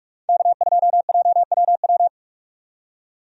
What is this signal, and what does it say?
Morse code, heard as a keyed tone, about 47 wpm, 700 Hz. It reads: K2JWW
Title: USA Call Signs - 5 Characters - 45wpm